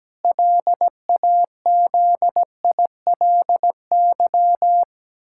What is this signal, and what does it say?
Morse code, heard as a keyed tone, about 17 wpm, 700 Hz. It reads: LAZILY
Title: Pangram Challenge v2 - 17wpm